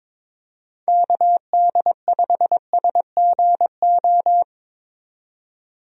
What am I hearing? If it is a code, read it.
Morse code, heard as a keyed tone, about 22 wpm, 700 Hz. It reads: KD5SGO
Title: USA Call Signs - 6 Characters - 22wpm